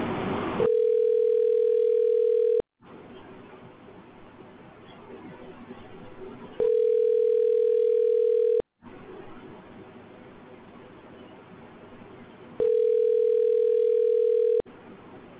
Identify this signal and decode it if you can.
Ringback tone (North America), 440 + 480 Hz.